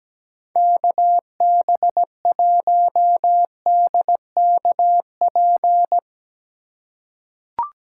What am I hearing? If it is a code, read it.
Morse code, heard as a keyed tone, about 17 wpm, 700 Hz. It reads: KB1DKP
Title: USA Call Signs - 6 Characters - 2x Character Spacing - 17wpm